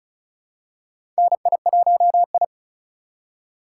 Morse code, heard as a keyed tone, about 36 wpm, 700 Hz. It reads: NI1I
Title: USA Call Signs - 4 Characters - 4x Character Spacing - 35wpm